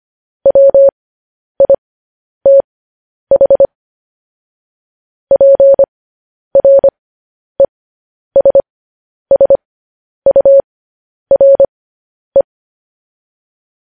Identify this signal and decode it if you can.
Morse code, heard as a keyed tone, about 25 wpm, spaced as for 10 wpm, 550 Hz. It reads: WITH PRESSURE
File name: podcast / News headlines-1x-25x10